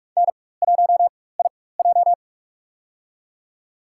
Morse code, heard as a keyed tone, about 46 wpm, spaced as for 21 wpm, 690 Hz. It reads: N1IJ
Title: USA Call Signs - 4x Character Spacing - 45wpm